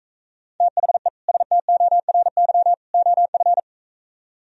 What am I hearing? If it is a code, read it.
Morse code, heard as a keyed tone, about 45 wpm, 700 Hz. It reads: THE STORY OF